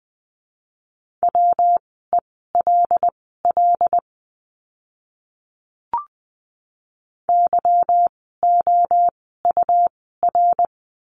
Morse code, heard as a keyed tone, about 20 wpm, spaced as for 14 wpm, 700 Hz. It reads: WELL YOUR
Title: 206 Lesson 33f - Letter j - Words 20wpm